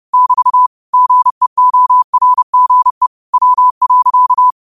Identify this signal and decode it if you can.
Morse code, heard as a keyed tone, about 30 wpm, 1000 Hz. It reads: <BT> GEORGE W.
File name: qotdfiles / 30wpm